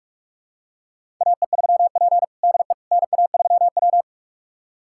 Morse code, heard as a keyed tone, about 42 wpm, 700 Hz. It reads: AE3P DE NA3W
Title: USA Callsign DE Callsign - 4 Characters - 45wpm